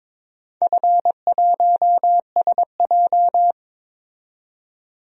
Morse code, heard as a keyed tone, about 22 wpm, 700 Hz. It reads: F1SJ
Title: DXCC-100 Call Signs - 4 Characters - 22wpm